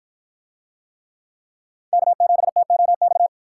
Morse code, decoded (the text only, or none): K7TOX